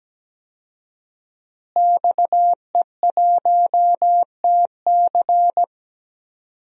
Morse code, heard as a keyed tone, about 17 wpm, 700 Hz. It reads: XE1TC
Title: DXCC-100 Call Signs - 5 Characters - Repeated 3x - 17wpm